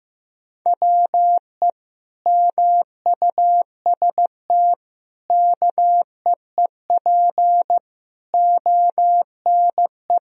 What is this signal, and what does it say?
Morse code, heard as a keyed tone, about 15 wpm, 700 Hz. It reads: WE MUST KEEP ONE
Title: Sentences from Top 1000 Words - Review Part 3 - 15wpm